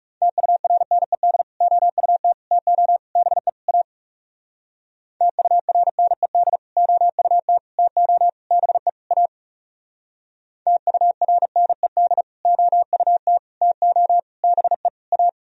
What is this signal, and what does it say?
Morse code, heard as a keyed tone, about 35 wpm, 700 Hz. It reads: TURNED OUT TO BE A TURNED OUT TO BE A TURNED OUT TO BE A
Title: Speed-Racing - Sets of 5 Words 30wpm